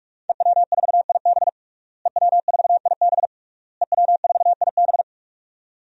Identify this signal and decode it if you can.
Morse code, heard as a keyed tone, about 46 wpm, 700 Hz. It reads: EW4IB EW4IB EW4IB
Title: DXCC-100 Call Signs - 5 Characters - Repeated 3x - 45wpm